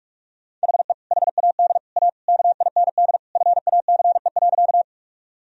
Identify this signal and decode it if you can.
Morse code, heard as a keyed tone, about 46 wpm, 700 Hz. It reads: HE HAD A KIND FACE.
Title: Sentences from Top 1000 Words - Review Part 2 - 45wpm